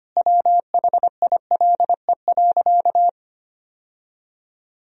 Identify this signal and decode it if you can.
Morse code, heard as a keyed tone, about 25 wpm, 700 Hz. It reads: WHILE.
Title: Sentences from Top 1000 Words - Review Part 1 - 25wpm